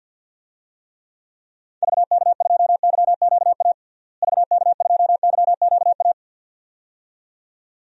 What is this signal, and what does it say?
Morse code, heard as a keyed tone, about 52 wpm, 700 Hz. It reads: VK2YQA VK2YQA
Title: DXCC-100 Call Signs - 6 Characters - Repeated 2x - 50wpm